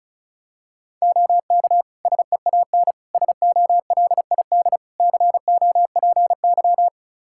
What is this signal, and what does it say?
Morse code, heard as a keyed tone, about 34 wpm, 700 Hz. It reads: OK SEAN SOLID COPY
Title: Practice QSO - Exchange 3 - More Info - 35wpm